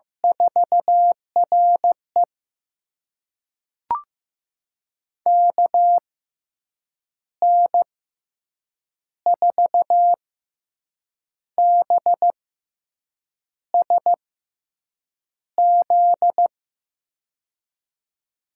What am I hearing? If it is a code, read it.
Morse code, heard as a keyed tone, about 16 wpm, spaced as for 6 wpm, 700 Hz. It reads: 4RE KN4BSZ
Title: USA Call Signs - 6x Character Spacing - 15wpm